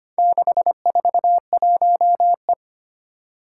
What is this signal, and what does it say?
Morse code, heard as a keyed tone, about 25 wpm, 690 Hz. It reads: 641E
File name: practice / 25wpm / Four Letter-Number Combinations - 25wpm